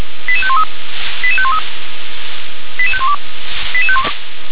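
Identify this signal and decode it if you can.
Five-tone selective call (ZVEI), heard as quick runs of five tones, tones of about 70 ms. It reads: 88513 88513 88512
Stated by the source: Received on an SDR.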